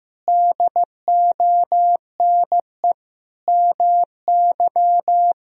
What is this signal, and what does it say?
Morse code, heard as a keyed tone, about 16 wpm, 700 Hz. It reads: DONE MY
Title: Sentences from Top 1000 Words - Review Part 4 - 15wpm